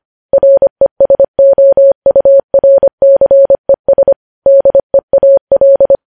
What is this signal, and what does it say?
Morse code, heard as a keyed tone, about 25 wpm, 550 Hz. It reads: RESOURCES DEAL
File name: podcast / News headlines-1x-25x25